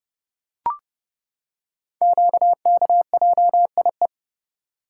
Morse code, heard as a keyed tone, about 30 wpm, 700 Hz. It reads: QKJIE
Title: Code Groups - 5 Letters - 30wpm